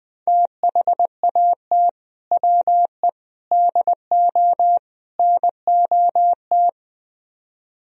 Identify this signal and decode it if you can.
Morse code, heard as a keyed tone, about 22 wpm, 700 Hz. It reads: THAT WE DO NOT
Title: Speed-Racing - Sets of 4 Words 20wpm